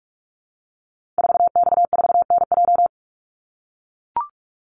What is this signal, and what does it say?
Morse code, heard as a keyed tone, about 44 wpm, 700 Hz. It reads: 4X4NJ
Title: Call Signs - 45wpm